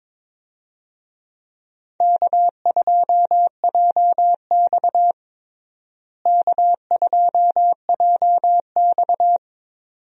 Morse code, heard as a keyed tone, about 22 wpm, 700 Hz. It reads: K2JX K2JX
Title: USA Call Signs - 4 Characters - Repeated 2x - 22wpm